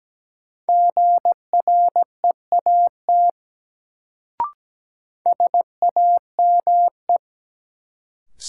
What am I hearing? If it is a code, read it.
Morse code, heard as a keyed tone, about 17 wpm, 700 Hz. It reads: GREAT SAME
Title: Sets of 2 Words - New Word Drill - 17wpm